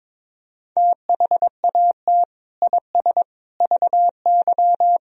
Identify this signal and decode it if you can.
Morse code, heard as a keyed tone, about 22 wpm, 700 Hz. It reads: THAT IS VY
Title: Sentences from Top 400 Words and Abbreviations 22wpm